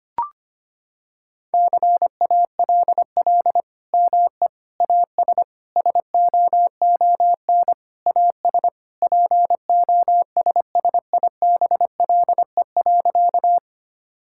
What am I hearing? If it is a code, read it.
Morse code, heard as a keyed tone, about 25 wpm, 700 Hz. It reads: CALL ME AS SOON AS POSSIBLE.
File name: practice / 25wpm / Sentences with 3-Word Phrases - 25wpm